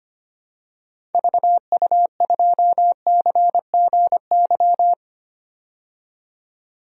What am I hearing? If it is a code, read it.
Morse code, heard as a keyed tone, about 25 wpm, 700 Hz. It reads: VU2CGY
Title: DXCC-100 Call Signs - 6 Characters - 25wpm